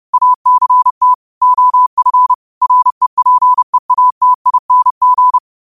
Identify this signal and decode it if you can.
Morse code, heard as a keyed tone, about 30 wpm, 1000 Hz. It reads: ACT OF REPEATING